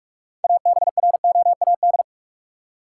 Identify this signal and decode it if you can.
Morse code, heard as a keyed tone, about 46 wpm, 700 Hz. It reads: ABROAD
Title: Six Letter Words - 45wpm